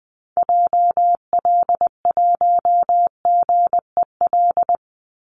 Morse code, heard as a keyed tone, about 20 wpm, 700 Hz. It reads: JL1GEL
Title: Call Signs - 20wpm